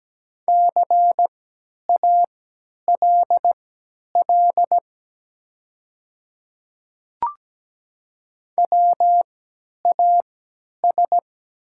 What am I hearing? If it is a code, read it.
Morse code, heard as a keyed tone, about 17 wpm, spaced as for 10 wpm, 700 Hz. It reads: CALL WAS
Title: 3x Character Spacing - Top 200 Words - 17wpm